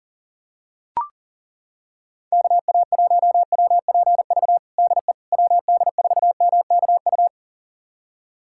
Morse code, heard as a keyed tone, about 40 wpm, 700 Hz. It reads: KA1WPV DE WD4MKU
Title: USA Callsign DE Callsign - 6 Characters - 40wpm